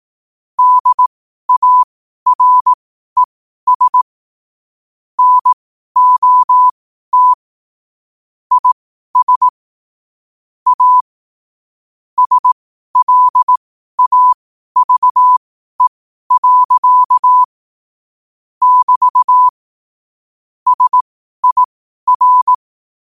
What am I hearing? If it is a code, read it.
Morse code, heard as a keyed tone, about 18 wpm, spaced as for 12 wpm, 1000 Hz. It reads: DARES NOT IS A SLAVE. <BT> SIR
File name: qotdfiles / 12wpm